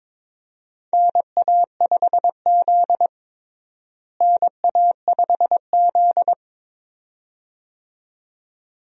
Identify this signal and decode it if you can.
Morse code, heard as a keyed tone, about 22 wpm, 700 Hz. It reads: NA5Z NA5Z
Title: USA Call Signs - 4 Characters - Repeated 2x - 22wpm